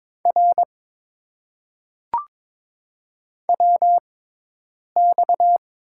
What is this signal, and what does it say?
Morse code, heard as a keyed tone, about 22 wpm, spaced as for 8 wpm, 700 Hz. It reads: R WX
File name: Top 90 QSO Elements - 6x Character Spacing - 22wpm